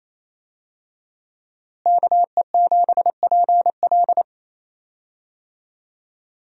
Morse code, heard as a keyed tone, about 28 wpm, 700 Hz. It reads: KE7PL
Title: USA Call Signs - 5 Characters - Repeated 2x - 28wpm